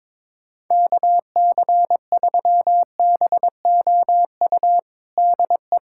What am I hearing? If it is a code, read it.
Morse code, heard as a keyed tone, about 21 wpm, 700 Hz. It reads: KC3BOU DE
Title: USA Callsign DE Callsign 22wpm